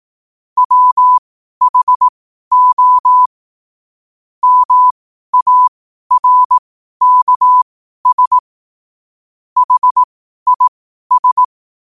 Morse code, heard as a keyed tone, about 18 wpm, spaced as for 12 wpm, 1000 Hz. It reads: WHO MARKS HIS
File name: qotdfiles / 12wpm